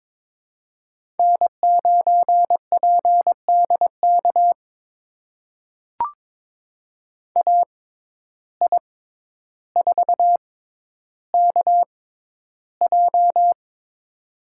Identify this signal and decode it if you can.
Morse code, heard as a keyed tone, about 23 wpm, spaced as for 11 wpm, 700 Hz. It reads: N9PDK AI4KJ
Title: USA Call Signs - 5 Characters - 6x Character Spacing - 22wpm